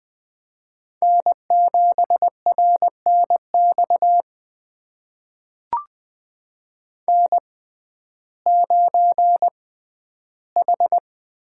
Morse code, heard as a keyed tone, about 21 wpm, spaced as for 12 wpm, 700 Hz. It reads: N7RNX N9H
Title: USA Call Signs - 5 Characters - 6x Character Spacing - 20wpm